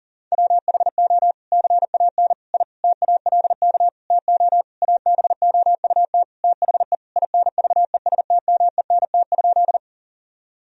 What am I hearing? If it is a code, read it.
Morse code, heard as a keyed tone, about 37 wpm, 700 Hz. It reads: WHO CAN I TALK TO ABOUT THE INVESTMENT?
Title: Sentences from Top 1000 Words - Review Part 4 - 40wpm